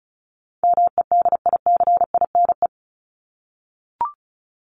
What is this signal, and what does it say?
Morse code, heard as a keyed tone, about 34 wpm, 700 Hz. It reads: MEDICINE
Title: Top 751-1000 Words 35wpm